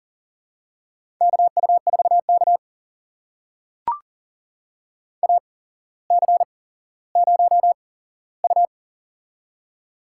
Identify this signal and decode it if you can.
Morse code, heard as a keyed tone, about 39 wpm, spaced as for 17 wpm, 690 Hz. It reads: KU4K AC0U
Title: USA Call Signs - 4 Characters - 8x Character Spacing - 40wpm